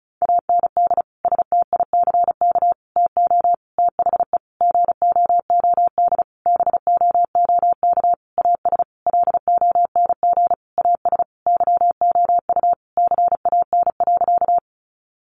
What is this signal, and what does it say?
Morse code, heard as a keyed tone, about 34 wpm, 700 Hz. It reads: AND STICK TO THE GOOD BOOK AS LONG AS YOU CAN.
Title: Sentences from Top 1200 Words 35wpm